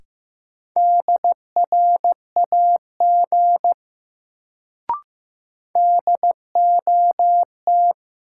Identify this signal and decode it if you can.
Morse code, heard as a keyed tone, about 15 wpm, 700 Hz. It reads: DRAG DOT
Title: Sentences from Top 1500 Words - New Word Drill - 15wpm